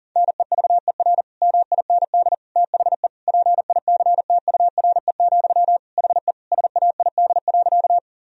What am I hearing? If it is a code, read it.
Morse code, heard as a keyed tone, about 40 wpm, 700 Hz. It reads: NEVER MIND THE PICTURE, HE SAID.
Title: Sentences from Top 1000 Words - Review Part 1 - 40wpm